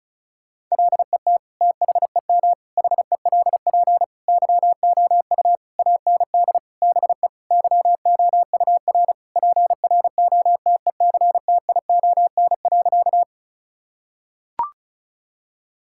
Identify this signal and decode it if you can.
Morse code, heard as a keyed tone, about 36 wpm, 700 Hz. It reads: LET THEM HELP YOU AND BE YOUR PROTECTION.
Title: Sentences from Top 1600 Words 35wpm